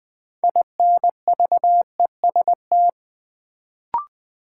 Morse code, heard as a keyed tone, about 21 wpm, 700 Hz. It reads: INVEST
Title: Sentences from Top 1100 Words - New Word Drill Alt - 20wpm